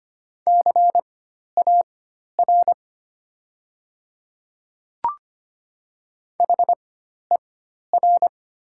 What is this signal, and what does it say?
Morse code, heard as a keyed tone, about 25 wpm, spaced as for 12 wpm, 690 Hz. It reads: CAR HER
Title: 4x Character Spacing - Top 500 Words - 25wpm